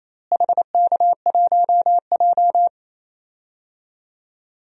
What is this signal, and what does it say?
Morse code, heard as a keyed tone, about 28 wpm, 700 Hz. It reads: HK1J
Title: DXCC-100 Call Signs - 4 Characters - Repeated 2x - 28wpm